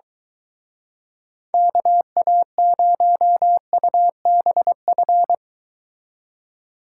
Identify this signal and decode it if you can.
Morse code, heard as a keyed tone, about 23 wpm, 700 Hz. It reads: KA0UBF